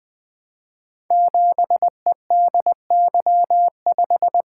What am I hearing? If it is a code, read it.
Morse code, heard as a keyed tone, about 20 wpm, 700 Hz. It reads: 7EDY5
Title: Code Groups - 5 Letter:Numbers - 20wpm